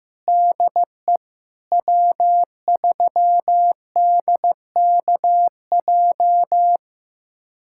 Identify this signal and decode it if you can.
Morse code, heard as a keyed tone, about 15 wpm, 700 Hz. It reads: DE W3DKJ
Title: USA Callsign DE Callsign - 5 Characters - 15wpm